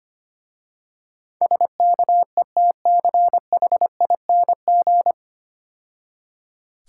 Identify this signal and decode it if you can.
Morse code, heard as a keyed tone, about 26 wpm, 700 Hz. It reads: SKETCHING
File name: Easy Words - Speed-Racing with Hints - 25wpm